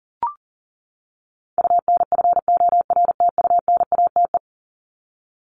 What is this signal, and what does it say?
Morse code, heard as a keyed tone, about 40 wpm, 700 Hz. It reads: UNFORTUNATE
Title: LY Suffix - Top 100 Words with Base Words 40wpm